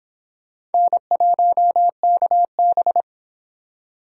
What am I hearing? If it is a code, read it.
Morse code, heard as a keyed tone, about 30 wpm, 700 Hz. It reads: N1KB